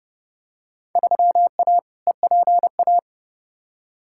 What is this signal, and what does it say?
Morse code, heard as a keyed tone, about 31 wpm, 700 Hz. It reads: 3A EPA
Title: Field Day Exchanges - 30wpm